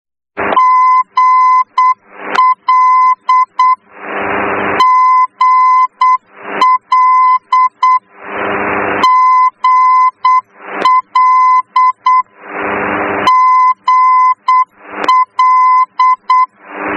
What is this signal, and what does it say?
Morse code, heard as a keyed tone, about 8 wpm, 1020 Hz. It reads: GL GL GL GL
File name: germany / mx / GL-D